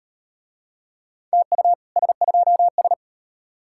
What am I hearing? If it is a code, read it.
Morse code, heard as a keyed tone, about 38 wpm, 700 Hz. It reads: TU S2S